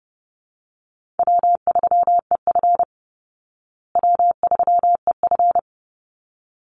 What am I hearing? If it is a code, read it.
Morse code, heard as a keyed tone, about 30 wpm, 700 Hz. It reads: W3EF W3EF
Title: Call Signs Repeated 3x - 30wpm